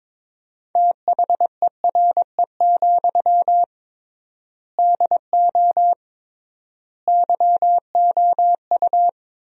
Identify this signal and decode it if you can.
Morse code, heard as a keyed tone, about 22 wpm, 700 Hz. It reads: THERE, DO YOU
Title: 3x Word-Spacing - Sentences from Top 300 Words - 22wpm